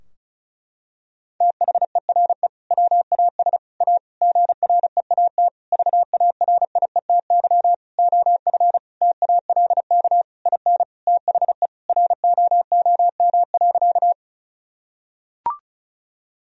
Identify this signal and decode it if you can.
Morse code, heard as a keyed tone, about 36 wpm, 700 Hz. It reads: THERE WAS A GREAT VARIETY OF TALK IN THE ROOM.
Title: Sentences from Top 1000 Words - Review Part 4 - 35wpm